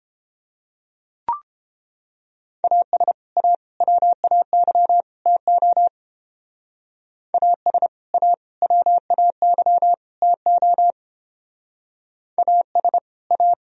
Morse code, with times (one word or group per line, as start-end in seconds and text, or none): 2.6-3.1 AS
3.4-3.5 A
3.8-5.0 WAY
5.3-5.9 TO
7.3-7.9 AS
8.1-8.3 A
8.6-9.9 WAY
10.2-10.9 TO
12.4-13.0 AS
13.3-13.5 A